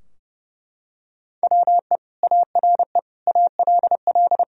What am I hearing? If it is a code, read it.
Morse code, heard as a keyed tone, about 32 wpm, 700 Hz. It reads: WE ARE ALL
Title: Sentences from Top 1000 Words - Review Part 3 - 30wpm